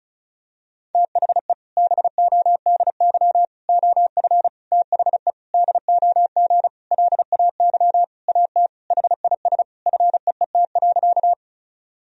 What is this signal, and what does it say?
Morse code, heard as a keyed tone, about 36 wpm, 700 Hz. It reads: THE BODY OF THE DOG LAY AT HIS FEET.
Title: Sentences from Top 1000 Words - Review Part 2 - 35wpm